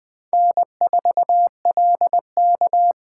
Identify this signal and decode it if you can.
Morse code, heard as a keyed tone, about 20 wpm, 700 Hz. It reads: N4LK
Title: Speed Racing - USA 1x2 Call Signs 17wpm